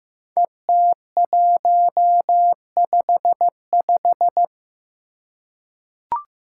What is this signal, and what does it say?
Morse code, heard as a keyed tone, about 15 wpm, 700 Hz. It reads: ET155
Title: Five Challenging Letter-Number Combinations 15wpm